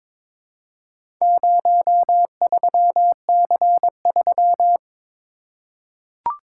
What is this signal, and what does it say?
Morse code, heard as a keyed tone, about 22 wpm, 700 Hz. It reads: 03C3
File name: practice / 22wpm / Four Letter-Number Combinations - 22wpm